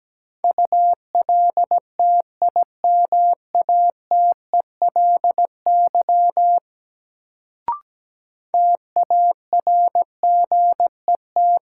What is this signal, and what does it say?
Morse code, heard as a keyed tone, about 17 wpm, 700 Hz. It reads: ULTIMATELY TARGET
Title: Sentences from Top 1200 Words - New Word Drill - 17wpm